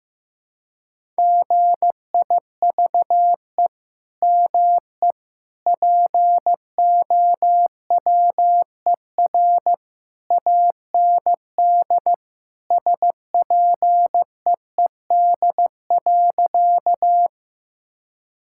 Morse code, heard as a keyed tone, about 15 wpm, 700 Hz. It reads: GIVE ME POWER AND SPEED.
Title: Sentences from Top 1000 Words - Review Part 5 - 15wpm